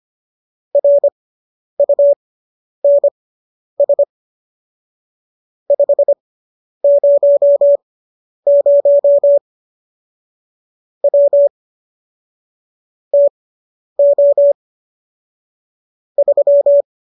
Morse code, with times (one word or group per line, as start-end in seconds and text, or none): 0.8-4.0 RUNS
5.7-9.4 500
11.0-11.5 W
13.1-14.5 TO
16.2-16.8 3